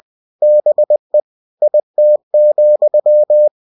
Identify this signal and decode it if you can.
Morse code, heard as a keyed tone, about 20 wpm, 600 Hz. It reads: BE IT,
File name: cw-books / winnie-the-pooh / winnie-20wpm-page-0036